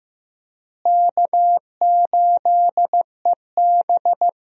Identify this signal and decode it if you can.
Morse code, heard as a keyed tone, about 15 wpm, 700 Hz. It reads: K8EB
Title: USA Call Signs - 4 Characters - Repeated 2x - 15wpm